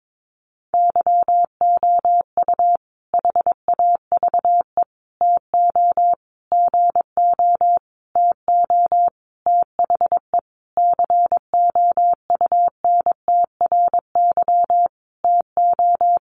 Morse code, heard as a keyed tone, about 22 wpm, 700 Hz. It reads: YOU HAVE TO GO TO THE COUNTRY TO
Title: Sentences from Top 1300 Words 22wpm